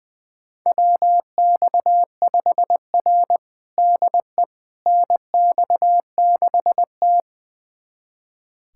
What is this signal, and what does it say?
Morse code, heard as a keyed tone, about 20 wpm, 700 Hz. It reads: WX5R DE NX6T
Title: USA Callsign DE Callsign - 4 Characters - 20wpm